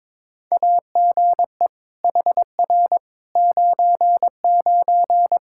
Morse code, heard as a keyed tone, about 22 wpm, 700 Hz. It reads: AGE HR 99
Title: Practice QSO - Exchange 3 - More Info - 22wpm